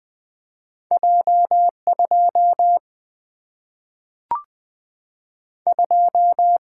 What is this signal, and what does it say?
Morse code, heard as a keyed tone, about 20 wpm, 700 Hz. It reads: J 2 2
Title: Two Letter-Number Combinations - 20wpm